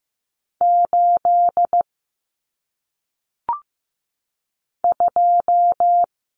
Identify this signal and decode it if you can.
Morse code, heard as a keyed tone, about 15 wpm, 700 Hz. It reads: 8 2
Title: Common Cut Numbers - 1-Digit 15wpm